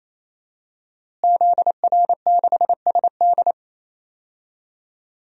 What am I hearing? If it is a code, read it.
Morse code, heard as a keyed tone, about 28 wpm, 700 Hz. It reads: ZR6SD
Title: DXCC-100 Call Signs - 5 Characters - Repeated 3x - 28wpm